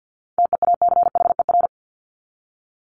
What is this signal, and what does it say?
Morse code, heard as a keyed tone, about 50 wpm, 700 Hz. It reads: TEACHER
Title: Top 751-1000 Words 50wpm